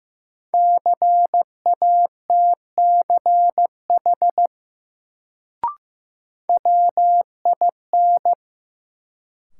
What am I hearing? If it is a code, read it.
Morse code, heard as a keyed tone, about 15 wpm, 700 Hz. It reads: CATCH WIN
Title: Sentences from Top 1500 Words - New Word Drill Alt - 15wpm